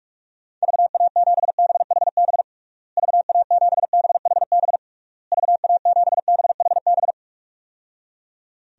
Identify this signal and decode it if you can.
Morse code, heard as a keyed tone, about 44 wpm, 700 Hz. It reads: VA7BHB VA7BHB VA7BHB
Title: DXCC-100 Call Signs - 6 Characters - Repeated 3x - 45wpm